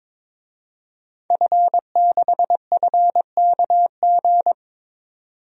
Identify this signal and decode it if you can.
Morse code, heard as a keyed tone, about 22 wpm, 700 Hz. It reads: F6FKG
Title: DXCC-100 Call Signs - 5 Characters - 22wpm